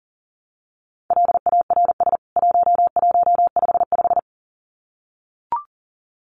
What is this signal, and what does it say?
Morse code, heard as a keyed tone, about 40 wpm, 700 Hz. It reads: LARS 1155
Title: CWT Exchange 40wpm